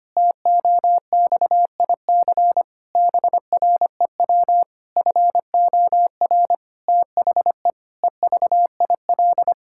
Morse code, heard as a keyed tone, about 25 wpm, 700 Hz. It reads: TOXIC BREW FOR THE EVIL
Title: Pangram Challenge v2 - 25wpm